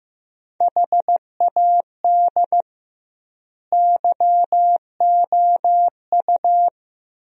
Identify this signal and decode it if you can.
Morse code, heard as a keyed tone, about 15 wpm, 700 Hz. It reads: HAD YOU
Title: 2x Word-Spacing - Sentences from Top 400 Words - 15wpm